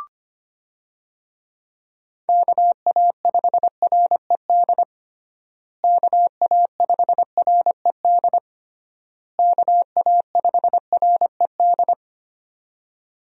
Morse code, heard as a keyed tone, about 25 wpm, 700 Hz. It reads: KA5RED KA5RED KA5RED
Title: USA Call Signs - 6 Characters - Repeated 3x - 25wpm